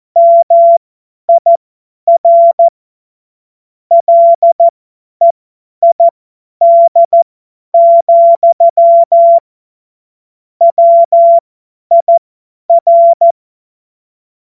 Morse code, heard as a keyed tone, about 14 wpm, spaced as for 10 wpm, 680 Hz. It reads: MIR LEID, WIR
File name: Morsekurs / Witze / Witz 09 April 2024 50 F70